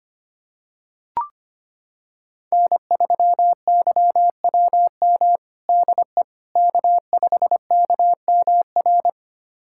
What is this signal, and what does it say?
Morse code, heard as a keyed tone, about 25 wpm, 700 Hz. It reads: N3YWM DE K5KMR
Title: USA Callsign DE Callsign - 5 Characters - 25wpm